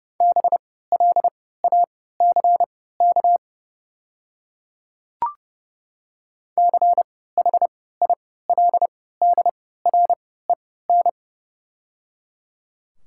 Morse code, heard as a keyed tone, about 31 wpm, spaced as for 17 wpm, 700 Hz. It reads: BLACK CHILDREN
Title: 3x Character Spacing - Top 500 Words - 30wpm